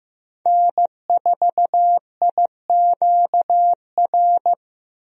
Morse code, heard as a keyed tone, about 15 wpm, 700 Hz. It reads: N4IQR
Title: USA Call Signs - 5 Characters - 15wpm